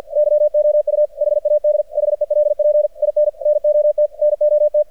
Morse code, heard as a keyed tone, about 53 wpm, 600 Hz. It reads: YOU HAD HELP IN NOT NOT
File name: archived / CRSnet